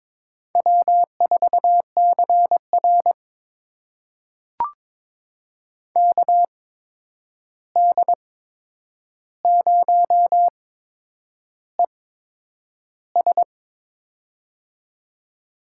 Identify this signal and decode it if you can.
Morse code, heard as a keyed tone, about 21 wpm, spaced as for 9 wpm, 700 Hz. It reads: W4CR KD0ES
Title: USA Call Signs - 8x Character Spacing - 22wpm